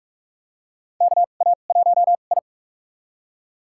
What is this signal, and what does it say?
Morse code, heard as a keyed tone, about 46 wpm, spaced as for 33 wpm, 700 Hz. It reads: KA1I
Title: USA Call Signs - 4 Characters - 2x Character Spacing - 45wpm